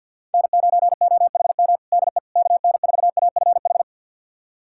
Morse code, heard as a keyed tone, about 52 wpm, 700 Hz. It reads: N9OHM DE KN4AFH
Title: USA Callsign DE Callsign 50wpm